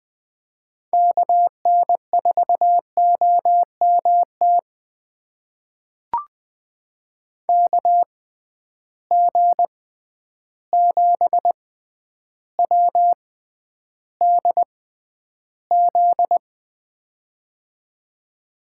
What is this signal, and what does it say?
Morse code, heard as a keyed tone, about 20 wpm, spaced as for 10 wpm, 700 Hz. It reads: KN4OMT KG7WDZ
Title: USA Call Signs - 6 Characters - 6x Character Spacing - 20wpm